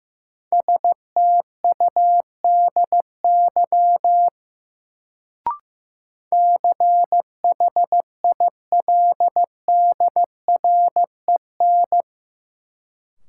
Morse code, heard as a keyed tone, about 15 wpm, 700 Hz. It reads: STUDY CHILDREN
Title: Sentences from Top 300 Words - New Word Drill - 15wpm